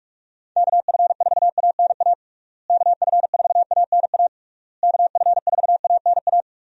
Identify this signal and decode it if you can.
Morse code, heard as a keyed tone, about 46 wpm, 700 Hz. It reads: KF4ANA KF4ANA KF4ANA
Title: USA Call Signs - 6 Characters - Repeated 3x - 45wpm